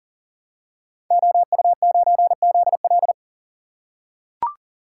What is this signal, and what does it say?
Morse code, heard as a keyed tone, about 40 wpm, 700 Hz. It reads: OU9ZL
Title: Code Groups - 5 Letter:Numbers - 40wpm